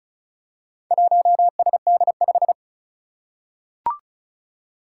Morse code, heard as a keyed tone, about 36 wpm, 690 Hz. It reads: 1SD5
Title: Four Challenging Letter-Number Combinations 35wpm